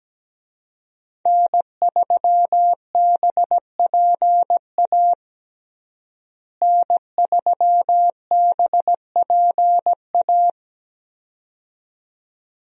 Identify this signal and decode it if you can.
Morse code, heard as a keyed tone, about 17 wpm, 700 Hz. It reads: N3BPA N3BPA
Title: USA Call Signs - 5 Characters - Repeated 2x - 17wpm